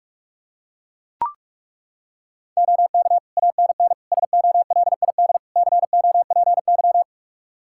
Morse code, heard as a keyed tone, about 42 wpm, 700 Hz. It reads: OK ANN SOLID COPY
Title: K1USN SST QSO Practice 45wpm